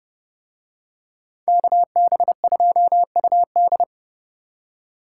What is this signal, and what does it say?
Morse code, heard as a keyed tone, about 28 wpm, 700 Hz. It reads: KB2UD